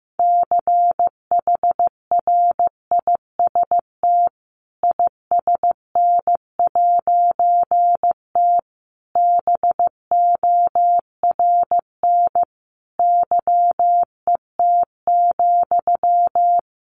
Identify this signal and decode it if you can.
Morse code, heard as a keyed tone, about 15 wpm, 700 Hz. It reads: CHRIST ISN'T BORN YET,
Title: A Sound Of Thunder 15wpm